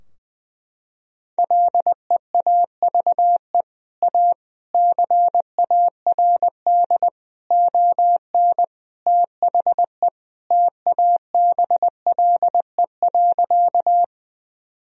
Morse code, heard as a keyed tone, about 21 wpm, 700 Hz. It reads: LEAVE A CARD ON THE TABLE.
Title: Sentences from Top 1000 Words - Review Part 4 - 20wpm